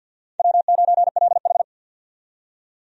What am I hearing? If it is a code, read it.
Morse code, heard as a keyed tone, about 47 wpm, 700 Hz. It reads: W9LH
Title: USA Call Signs - 4 Characters - 50wpm